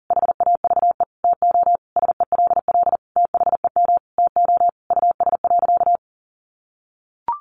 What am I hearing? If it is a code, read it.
Morse code, heard as a keyed tone, about 40 wpm, 700 Hz. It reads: HAVE TO SELL THEM TO US.
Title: Sentences from Top 1100 Words 40wpm